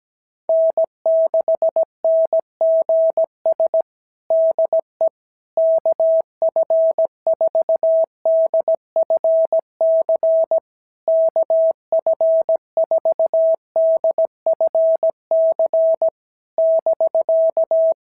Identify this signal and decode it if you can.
Morse code, heard as a keyed tone, about 18 wpm, 650 Hz. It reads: N6NGS DE KF4DFC KF4DFC <BK>
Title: SOTA QSO Practice 17wpm